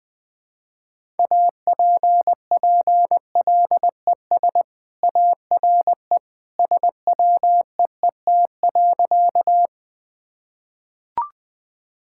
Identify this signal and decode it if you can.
Morse code, heard as a keyed tone, about 20 wpm, 700 Hz. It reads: APPLES ARE SWEET.
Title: Easy Sentences with Hints - 20wpm